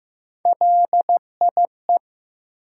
Morse code, heard as a keyed tone, about 15 wpm, 700 Hz. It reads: LIE
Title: Sentences from Top 1600 Words - New Word Drill - 15wpm